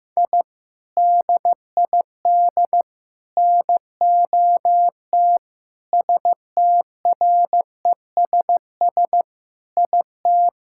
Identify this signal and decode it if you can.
Morse code, heard as a keyed tone, about 15 wpm, 700 Hz. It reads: I DID NOT STRESS IT
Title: Sentences from Top 1000 Words - Review Part 6 - 15wpm